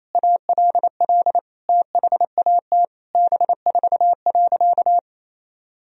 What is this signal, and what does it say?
Morse code, heard as a keyed tone, about 28 wpm, 700 Hz. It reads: ALL THAT B4.
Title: Sentences from Top 400 Words and Abbreviations 28wpm